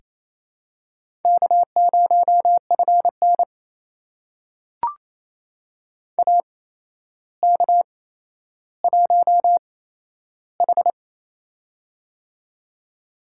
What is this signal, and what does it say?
Morse code, heard as a keyed tone, about 28 wpm, spaced as for 12 wpm, 700 Hz. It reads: K0FN AK1H
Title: USA Call Signs - 4 Characters - 8x Character Spacing - 28wpm